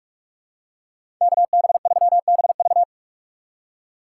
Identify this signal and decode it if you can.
Morse code, heard as a keyed tone, about 46 wpm, 700 Hz. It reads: KB3BV
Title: USA Call Signs - 5 Characters - 3x Character Spacing - 45wpm